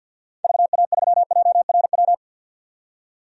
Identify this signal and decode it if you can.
Morse code, heard as a keyed tone, about 53 wpm, 700 Hz. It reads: VA3JRW
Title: DXCC-100 Call Signs - 6 Characters - 50wpm